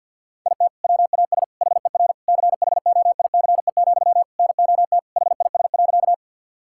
Morse code, heard as a keyed tone, about 50 wpm, 700 Hz. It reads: IT WAS HER CHOICE, NOT HIS.
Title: Sentences from Top 1000 Words - Review Part 6 - 50wpm